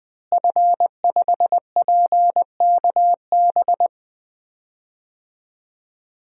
Morse code, heard as a keyed tone, about 20 wpm, 700 Hz. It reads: F5PKB
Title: DXCC-100 Call Signs - 5 Characters - Repeated 3x - 20wpm